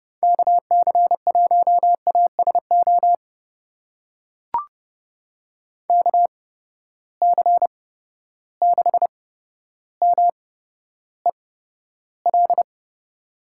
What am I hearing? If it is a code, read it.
Morse code, heard as a keyed tone, about 30 wpm, spaced as for 13 wpm, 700 Hz. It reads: KC1ASO KC6MEL
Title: USA Call Signs - 6 Characters - 8x Character Spacing - 30wpm